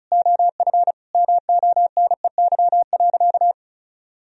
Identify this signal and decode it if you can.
Morse code, heard as a keyed tone, about 36 wpm, 700 Hz. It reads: OF MONEY.
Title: Sentences from Top 1000 Words - Review Part 4 - 35wpm